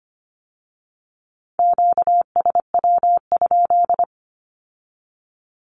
Morse code, heard as a keyed tone, about 25 wpm, 700 Hz. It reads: QSW?
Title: Q-Codes with Spoken Meaning - 25wpm